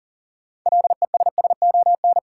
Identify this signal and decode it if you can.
Morse code, heard as a keyed tone, about 40 wpm, 700 Hz. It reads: LESSON
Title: Sentences from Top 1200 Words - New Word Drill - 40wpm